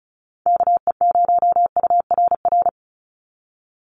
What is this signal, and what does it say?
Morse code, heard as a keyed tone, about 34 wpm, 700 Hz. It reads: KE0URR
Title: USA Call Signs 35wpm